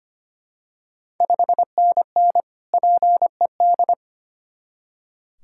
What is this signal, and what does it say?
Morse code, heard as a keyed tone, about 25 wpm, 700 Hz. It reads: 5NN PED
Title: Ontario QSO Party Exchange 25wpm